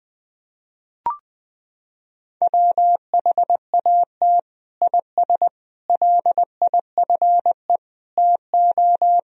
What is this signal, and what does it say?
Morse code, heard as a keyed tone, about 21 wpm, 700 Hz. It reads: WHAT IS LIFE TO
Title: Sentences from Top 300 Words and Abbreviations 20wpm